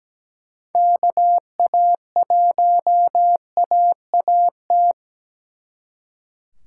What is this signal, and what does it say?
Morse code, heard as a keyed tone, about 20 wpm, 700 Hz. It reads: KA1AAT
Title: Speed Racing - USA 2x3 Call Signs 17wpm